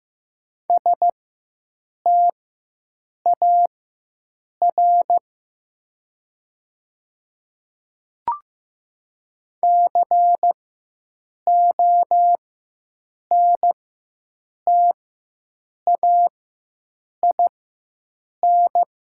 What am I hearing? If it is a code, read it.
Morse code, heard as a keyed tone, about 15 wpm, spaced as for 7 wpm, 700 Hz. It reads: STAR CONTAIN
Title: 4x Character Spacing - Top 500 Words - 15wpm